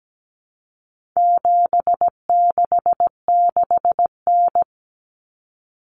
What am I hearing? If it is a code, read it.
Morse code, heard as a keyed tone, about 17 wpm, 700 Hz. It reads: 766N
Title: Common Cut Numbers - 4-Digit 17wpm